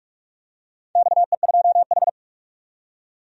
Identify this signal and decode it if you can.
Morse code, heard as a keyed tone, about 43 wpm, 700 Hz. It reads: XE2H
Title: DXCC-100 Call Signs - 4 Characters - 45wpm